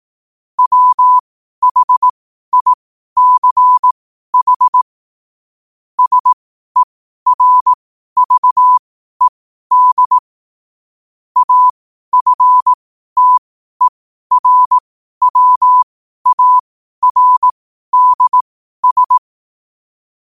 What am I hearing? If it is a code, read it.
Morse code, heard as a keyed tone, about 18 wpm, spaced as for 12 wpm, 1000 Hz. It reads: WHICH SERVED AFTERWARDS
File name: qotdfiles / 12wpm